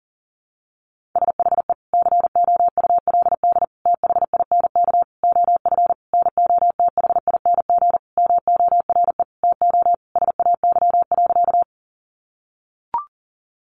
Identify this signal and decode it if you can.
Morse code, heard as a keyed tone, about 40 wpm, 700 Hz. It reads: SHE COULD THINK OF NOTHING MORE TO SAY.
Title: Sentences from Top 500 Words 40wpm